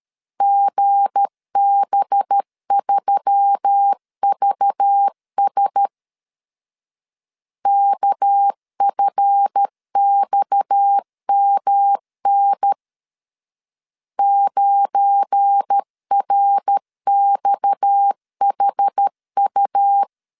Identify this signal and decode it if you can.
Morse code, heard as a keyed tone, about 13 wpm, 800 Hz. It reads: GB3VS KFXMN 9RXHU
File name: LEZIONI / 7 GF9Z / 7 GF9 12wpm 2319